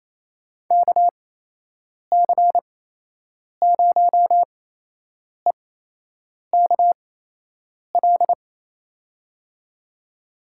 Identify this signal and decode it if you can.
Morse code, heard as a keyed tone, about 28 wpm, spaced as for 8 wpm, 700 Hz. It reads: KC0EKL
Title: USA Call Signs - 8x Character Spacing - 28wpm